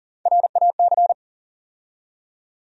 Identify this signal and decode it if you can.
Morse code, heard as a keyed tone, about 40 wpm, 700 Hz. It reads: RAC
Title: USA Call Sign Postfixes - 3 Letters 40wpm